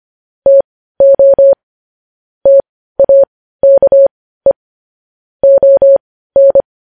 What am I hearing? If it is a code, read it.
Morse code, heard as a keyed tone, about 25 wpm, spaced as for 15 wpm, 550 Hz. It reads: TO TAKE ON
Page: News Headlines in Morse Code at 15 WPM